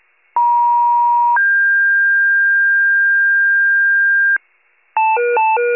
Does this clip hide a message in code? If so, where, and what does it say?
Yes: two-tone page from 0.4 to 4.4 s, heard as a one-second tone, then a three-second tone, about 950 then 1600 Hz, with noise.